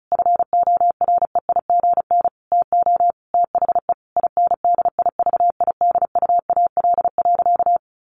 Morse code, heard as a keyed tone, about 34 wpm, 700 Hz. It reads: FOREIGN TO THE INDIVIDUAL.
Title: Sentences from Top 1400 Words 35wpm